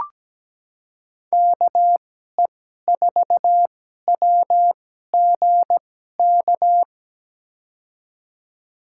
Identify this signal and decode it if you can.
Morse code, heard as a keyed tone, about 17 wpm, spaced as for 12 wpm, 700 Hz. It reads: KE4WGK
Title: USA Call Signs - 6 Characters - 2x Character Spacing - 17wpm